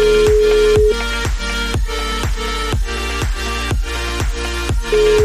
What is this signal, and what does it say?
Ringback tone (Europe, 425 Hz), with noise.